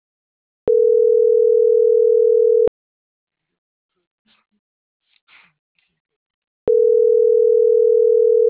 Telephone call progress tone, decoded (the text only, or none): ringback tone